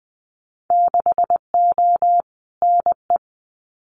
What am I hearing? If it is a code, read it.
Morse code, heard as a keyed tone, about 20 wpm, 700 Hz. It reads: 6O NE
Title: Winter Field Day Exchanges Encore 20wpm